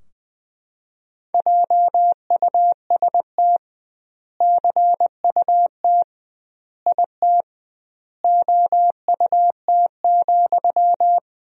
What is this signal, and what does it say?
Morse code, heard as a keyed tone, about 20 wpm, 700 Hz. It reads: JUST CUT IT OUT,
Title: 2x Word-Spacing - Sentences from Top 300 Words - 20wpm